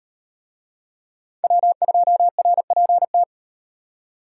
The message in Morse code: W2RPT

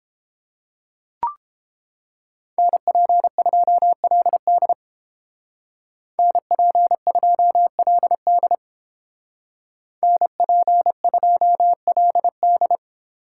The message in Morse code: NP2LD NP2LD NP2LD

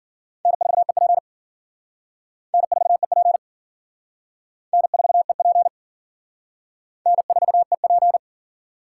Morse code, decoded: N4EP N4EP N4EP N4EP

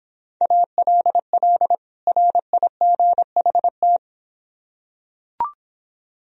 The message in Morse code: ALL RIGHT

